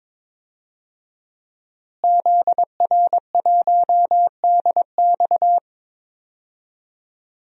Morse code, heard as a keyed tone, about 22 wpm, 700 Hz. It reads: ZR1DX
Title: DXCC-100 Call Signs - 5 Characters - Repeated 3x - 22wpm